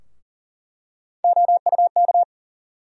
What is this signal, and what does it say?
Morse code, heard as a keyed tone, about 40 wpm, 700 Hz. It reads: OUK
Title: USA Call Sign Postfixes - 3 Letters 40wpm